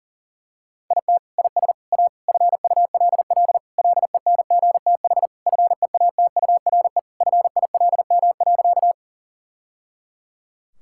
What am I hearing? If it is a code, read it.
Morse code, heard as a keyed tone, about 40 wpm, 700 Hz. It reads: IT IS A FULL LENGTH FEATURE FILM.
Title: Sentences from Top 1000 Words - Review Part 5 - 40wpm